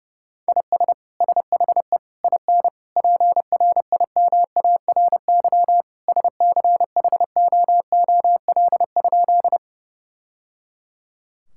Morse code, heard as a keyed tone, about 30 wpm, 700 Hz. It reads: IS SHE IN PRIMARY SCHOOL?
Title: Sentences from Top 1000 Words - Review Part 5 - 30wpm